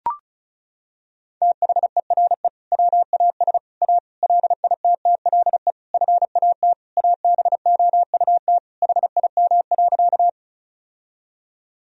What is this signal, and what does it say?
Morse code, heard as a keyed tone, about 34 wpm, 700 Hz. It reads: THERE WAS A LITTLE FAT ABOUT HIM.
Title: Sentences from Top 1000 Words - Review Part 4 - 35wpm